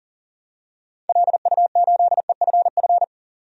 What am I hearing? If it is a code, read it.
Morse code, heard as a keyed tone, about 40 wpm, 700 Hz. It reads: LU8EFF
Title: DXCC-100 Call Signs - 6 Characters - 40wpm